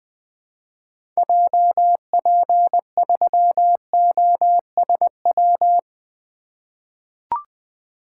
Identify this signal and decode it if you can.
Morse code, heard as a keyed tone, about 20 wpm, 700 Hz. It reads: JP3OSW
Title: 2023 SOTA Activator Call Signs - 20wpm